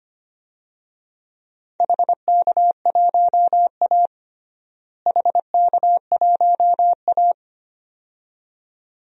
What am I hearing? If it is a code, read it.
Morse code, heard as a keyed tone, about 25 wpm, 700 Hz. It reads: HK1A HK1A
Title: DXCC-100 Call Signs - 4 Characters - Repeated 2x - 25wpm